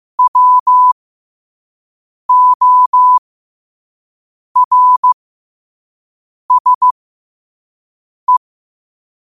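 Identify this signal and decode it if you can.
Morse code, heard as a keyed tone, about 15 wpm, spaced as for 5 wpm, 1000 Hz. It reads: WORSE
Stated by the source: Morse code sent at 5 wpm.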